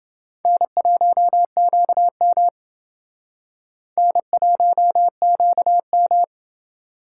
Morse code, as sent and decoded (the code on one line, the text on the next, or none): -. .---- --.- -- / -. .---- --.- --
N1QM N1QM